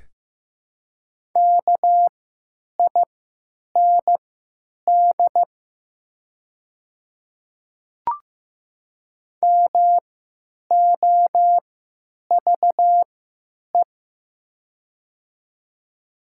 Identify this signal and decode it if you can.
Morse code, heard as a keyed tone, about 15 wpm, spaced as for 9 wpm, 700 Hz. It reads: KIND MOVE
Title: 3x Character Spacing - Top 200 Words - 15wpm